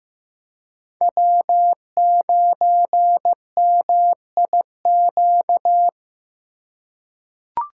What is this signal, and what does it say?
Morse code, heard as a keyed tone, about 15 wpm, 700 Hz. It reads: W9MIQ
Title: USA Call Signs - 2x Character Spacing - 15wpm